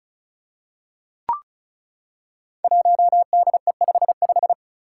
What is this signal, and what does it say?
Morse code, heard as a keyed tone, about 36 wpm, 700 Hz. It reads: 1DE55
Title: Five Challenging Letter-Number Combinations 35wpm